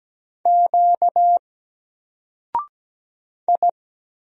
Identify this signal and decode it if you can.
Morse code, heard as a keyed tone, about 17 wpm, 700 Hz. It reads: Q I
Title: Single Letter-Number 17wpm